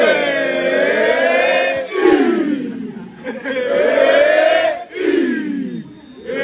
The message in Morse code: none